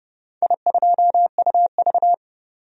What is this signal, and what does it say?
Morse code, heard as a keyed tone, about 30 wpm, 700 Hz. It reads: I2UV
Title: DXCC-100 Call Signs - 4 Characters - 30wpm